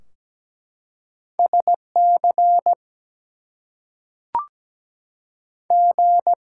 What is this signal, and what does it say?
Morse code, heard as a keyed tone, about 17 wpm, 700 Hz. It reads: SC G
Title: US States and Canadian Provinces 17wpm